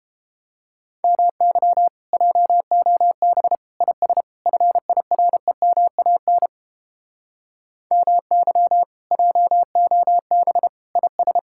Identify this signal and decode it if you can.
Morse code, heard as a keyed tone, about 32 wpm, 700 Hz. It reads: MY JOB IS FIREMAN MY JOB IS